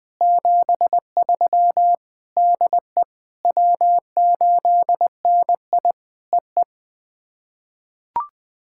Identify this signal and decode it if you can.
Morse code, heard as a keyed tone, about 20 wpm, 700 Hz. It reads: 73 DE W8NI EE
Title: POTA QSO Practice - 20wpm